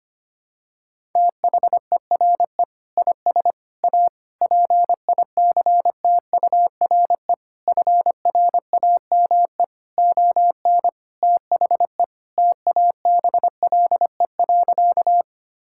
Morse code, heard as a keyed tone, about 25 wpm, 700 Hz. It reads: THERE IS A PICTURE FRAME ON THE TABLE.
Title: Sentences from Top 1000 Words - Review Part 6 - 25wpm